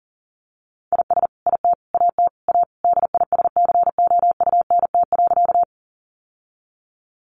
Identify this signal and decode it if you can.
Morse code, heard as a keyed tone, about 40 wpm, 700 Hz. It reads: IS IT AT A DISCOUNT.
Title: Sentences from Top 1200 Words 40wpm